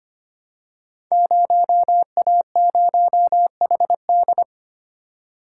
Morse code, heard as a keyed tone, about 25 wpm, 700 Hz. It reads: 0A0HD
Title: Code Groups - 5 Letter:Numbers - 25wpm